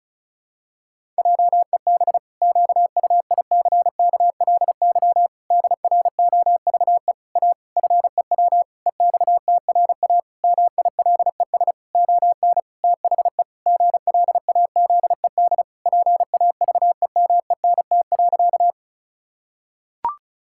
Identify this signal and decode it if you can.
Morse code, heard as a keyed tone, about 34 wpm, 700 Hz. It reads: JEB QUICKLY DROVE A FEW EXTRA MILES ON THE GLAZED PAVEMENT.
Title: Pangram Challenge v2 - 35wpm